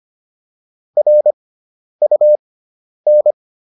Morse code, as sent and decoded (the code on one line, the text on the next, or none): .-. ..- -.
RUN